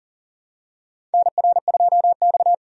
Morse code, heard as a keyed tone, about 40 wpm, 700 Hz. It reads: NR2X
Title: USA Call Signs - 4 Characters - 40wpm